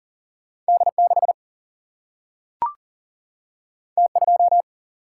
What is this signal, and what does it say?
Morse code, heard as a keyed tone, about 40 wpm, 700 Hz. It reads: D6 T2
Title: Single-Letter Single-Number 40wpm